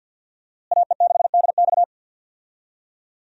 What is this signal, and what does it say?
Morse code, heard as a keyed tone, about 52 wpm, 700 Hz. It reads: AE6DX
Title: USA Call Signs - 5 Characters - Repeated 3x - 50wpm